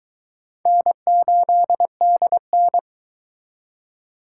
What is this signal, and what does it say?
Morse code, heard as a keyed tone, about 22 wpm, 700 Hz. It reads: N8DN